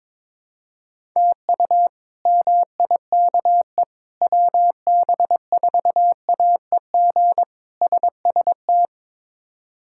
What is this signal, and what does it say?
Morse code, heard as a keyed tone, about 22 wpm, 700 Hz. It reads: TU MIKE WB4AEG SST
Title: K1USN SST QSO Practice - 22wpm